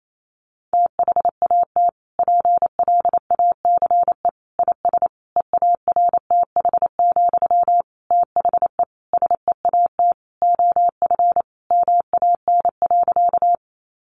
Morse code, heard as a keyed tone, about 28 wpm, 700 Hz. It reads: THAT PLACE IS EARTH, THE SEAT OF MAN.
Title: Sentences from Top 1500 Words 28wpm